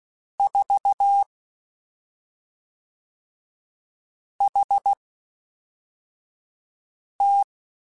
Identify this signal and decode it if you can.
Morse code, heard as a keyed tone, about 16 wpm, spaced as for 3 wpm, 800 Hz. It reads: 4HT
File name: CW Comparisons / CW Comparisons-mp3s / CW Comparisons Broken Characters @5